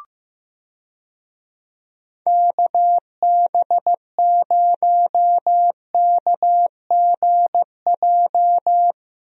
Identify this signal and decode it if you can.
Morse code, heard as a keyed tone, about 15 wpm, 700 Hz. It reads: KB0KGJ
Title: USA Call Signs - 6 Characters - Repeated 3x - 15wpm